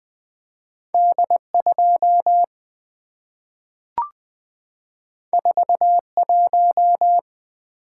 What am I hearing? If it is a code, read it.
Morse code, heard as a keyed tone, about 20 wpm, 700 Hz. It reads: D 2 4 1
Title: Two Letter-Number Combinations - 20wpm